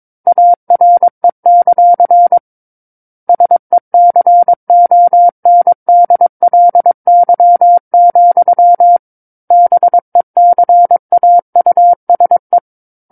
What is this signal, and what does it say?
Morse code, heard as a keyed tone, about 22 wpm, 700 Hz. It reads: ARE; SECONDLY, BECAUSE
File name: 06-Chapter-06 22 WPM